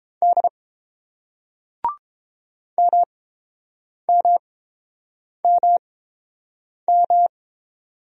Morse code, heard as a keyed tone, about 31 wpm, spaced as for 7 wpm, 690 Hz. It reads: D MMMM